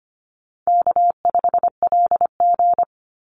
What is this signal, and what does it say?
Morse code, heard as a keyed tone, about 25 wpm, 700 Hz. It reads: K5LG
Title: Call Signs - 25wpm